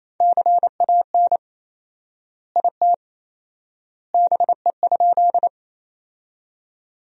Morse code, heard as a keyed tone, about 28 wpm, 700 Hz. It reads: CAN IT BE?
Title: 4x Word-Spacing - Sentences from Top 100 Words - 28wpm